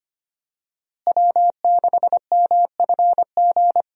Morse code, heard as a keyed tone, about 25 wpm, 700 Hz. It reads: W6MFG
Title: USA Call Signs - 5 Characters - Repeated 3x - 25wpm